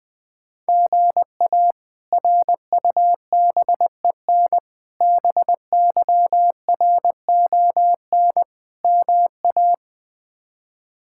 Morse code, heard as a keyed tone, about 21 wpm, 700 Hz. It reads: GA RUBEN BYRON MA
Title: K1USN SST QSO Practice - 20wpm